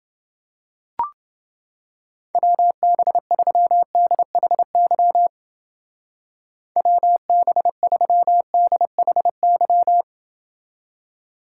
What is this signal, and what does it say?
Morse code, heard as a keyed tone, about 28 wpm, 700 Hz. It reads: WB3DHY WB3DHY